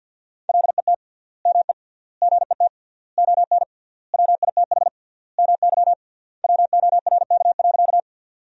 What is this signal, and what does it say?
Morse code, heard as a keyed tone, about 53 wpm, 700 Hz. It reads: LET ME GET ON WITH MY WORK.
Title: 3x Word-Spacing - Sentences from Top 300 Words - 50wpm